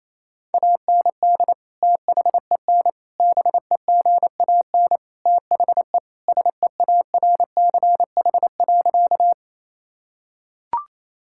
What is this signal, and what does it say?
Morse code, heard as a keyed tone, about 28 wpm, 700 Hz. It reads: AND THEN BEGAN THE SEARCH.
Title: Sentences from Top 1000 Words - Review Part 5 - 28wpm